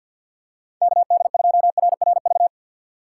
Morse code, heard as a keyed tone, about 53 wpm, 700 Hz. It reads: KD2RRV
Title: USA Call Signs - 6 Characters - 6x Character Spacing - 50wpm